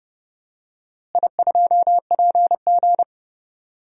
Morse code, heard as a keyed tone, about 30 wpm, 700 Hz. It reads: I2PG
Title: DXCC-100 Call Signs - 4 Characters - Repeated 3x - 30wpm